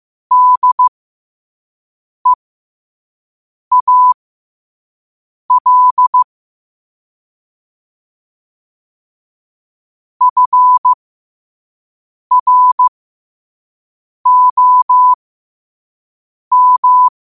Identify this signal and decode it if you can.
Morse code, heard as a keyed tone, about 15 wpm, spaced as for 5 wpm, 1000 Hz. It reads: DEAL FROM
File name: qotdfiles / 5wpm